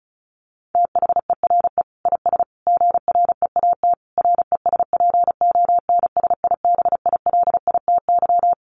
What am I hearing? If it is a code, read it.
Morse code, heard as a keyed tone, about 34 wpm, 700 Hz. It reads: THERE IS GREAT RESPONSIBILITY
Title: Sentences from Top 1200 Words 35wpm